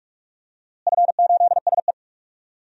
Morse code, heard as a keyed tone, about 43 wpm, 700 Hz. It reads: F8SE
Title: DXCC-100 Call Signs - 4 Characters - Repeated 2x - 45wpm